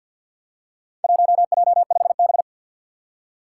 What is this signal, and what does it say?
Morse code, heard as a keyed tone, about 46 wpm, 700 Hz. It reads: 1J5B
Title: Four Challenging Letter-Number Combinations 50wpm